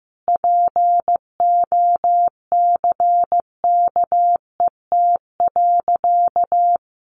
Morse code, heard as a keyed tone, about 15 wpm, 700 Hz. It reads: POCKET.
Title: A Sound Of Thunder 15wpm